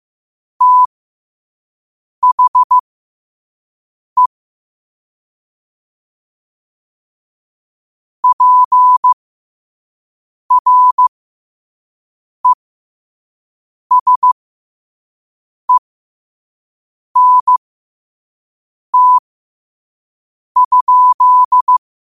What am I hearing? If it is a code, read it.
Morse code, heard as a keyed tone, about 15 wpm, spaced as for 5 wpm, 1000 Hz. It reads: THE PRESENT?
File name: qotdfiles / 5wpm